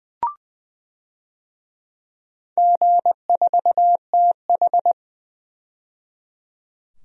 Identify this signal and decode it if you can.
Morse code, heard as a keyed tone, about 20 wpm, 700 Hz. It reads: G4TH
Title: DXCC-100 Call Signs - 4 Characters - 20wpm